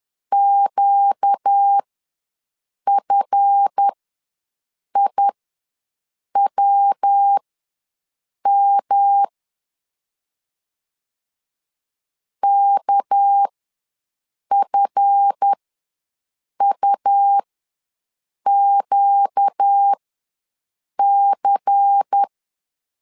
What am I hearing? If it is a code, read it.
Morse code, heard as a keyed tone, about 10 wpm, spaced as for 6 wpm, 800 Hz. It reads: QFIWM KFUQC
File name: LEZIONI / 7 GF9Z / 7 GF ws 10wpm 3918